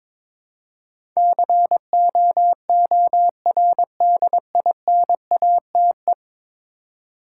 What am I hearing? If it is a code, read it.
Morse code, heard as a keyed tone, about 22 wpm, 700 Hz. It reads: COORDINATE
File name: practice / 22wpm / 10 Letter Words - First 3 Letters Given - 22wpm